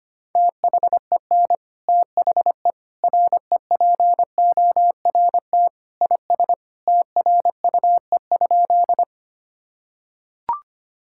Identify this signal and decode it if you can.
Morse code, heard as a keyed tone, about 25 wpm, 700 Hz. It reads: THEN THE REPORT IS TRUE?
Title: Sentences from Top 1000 Words - Review Part 6 - 25wpm